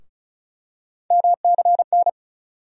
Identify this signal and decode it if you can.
Morse code, heard as a keyed tone, about 36 wpm, 700 Hz. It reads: MCN
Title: USA Call Sign Postfixes - 3 Letters 35wpm